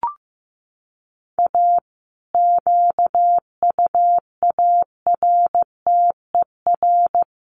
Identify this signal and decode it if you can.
Morse code, heard as a keyed tone, about 15 wpm, 700 Hz. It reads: A QUARTER
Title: Sets of 5 Words Encore2 15wpm